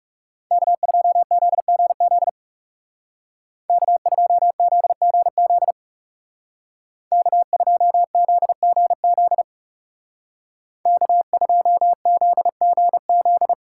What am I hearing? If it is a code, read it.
Morse code, heard as a keyed tone, about 39 wpm, 700 Hz. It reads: K2ZGZ K2ZGZ K2ZGZ K2ZGZ